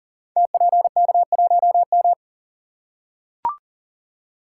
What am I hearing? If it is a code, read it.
Morse code, heard as a keyed tone, about 39 wpm, 700 Hz. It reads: TPK1M
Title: Code Groups - 5 Letter:Numbers - 40wpm